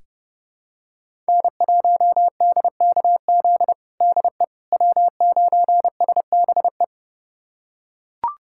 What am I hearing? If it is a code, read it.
Morse code, heard as a keyed tone, about 30 wpm, 700 Hz. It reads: N1DKZ DE W9SBE
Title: USA Callsign DE Callsign - 5 Characters - 30wpm